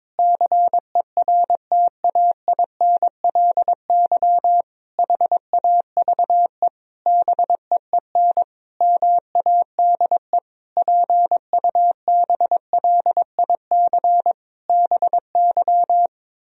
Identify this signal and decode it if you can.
Morse code, heard as a keyed tone, about 21 wpm, 700 Hz. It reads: CERTAINLY HAVE BEEN MADE PUBLIC BY
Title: Sentences from Top 1000 Words - Review Part 3 - 22wpm